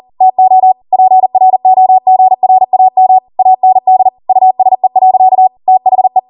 Morse code, heard as a keyed tone, about 40 wpm, 750 Hz. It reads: TO PROGRAM AND USE. THE
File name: Morse / 40WPM